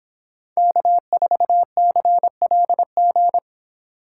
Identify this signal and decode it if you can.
Morse code, heard as a keyed tone, about 26 wpm, 700 Hz. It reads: K4CLG